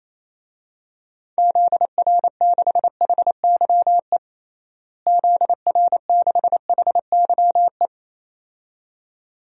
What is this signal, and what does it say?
Morse code, heard as a keyed tone, about 27 wpm, 700 Hz. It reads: ZR6HYE ZR6HYE
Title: DXCC-100 Call Signs - 6 Characters - Repeated 2x - 28wpm